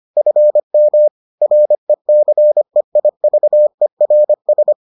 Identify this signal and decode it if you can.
Morse code, heard as a keyed tone, about 25 wpm, 600 Hz. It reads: FM RECEIVERS